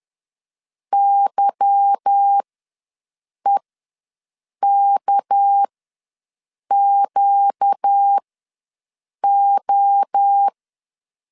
Morse code, heard as a keyed tone, about 10 wpm, spaced as for 6 wpm, 800 Hz. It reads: YEKQO